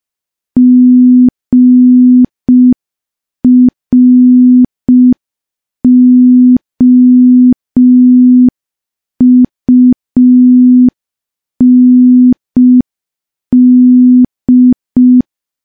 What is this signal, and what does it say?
Morse code, heard as a keyed tone, about 5 wpm, 250 Hz. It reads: GROUND
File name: morse / morse-training-05WPM